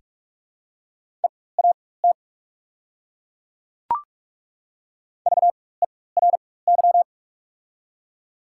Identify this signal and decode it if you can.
Morse code, heard as a keyed tone, about 48 wpm, spaced as for 21 wpm, 700 Hz. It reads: EAT VERY
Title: 4x Character Spacing - Top 300 Words - 45wpm